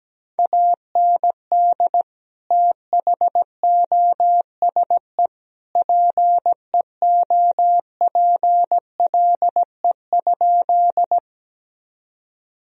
Morse code, heard as a keyed tone, about 17 wpm, 700 Hz. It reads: AND THOSE PEOPLE?
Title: Sentences from Top 1000 Words - Review Part 1 - 17wpm